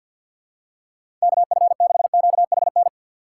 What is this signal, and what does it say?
Morse code, heard as a keyed tone, about 52 wpm, 700 Hz. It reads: KF6QHN
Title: USA Call Signs - 6x Character Spacing - 50wpm